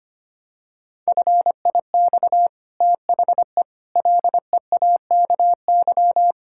Morse code, heard as a keyed tone, about 25 wpm, 700 Hz. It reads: FIX THE LEAKY
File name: Easy Sentences with Hints - 25wpm